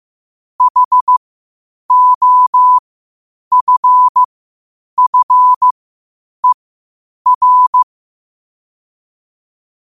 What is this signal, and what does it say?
Morse code, heard as a keyed tone, about 15 wpm, spaced as for 8 wpm, 1000 Hz. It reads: HOFFER
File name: qotdfiles / 8wpm